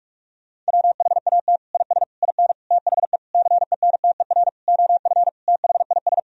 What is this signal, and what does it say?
Morse code, heard as a keyed tone, about 48 wpm, 700 Hz. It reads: WHAT IS IN THE CENTER OF THIS
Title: Sentences from Top 1000 Words - Review Part 2 - 45wpm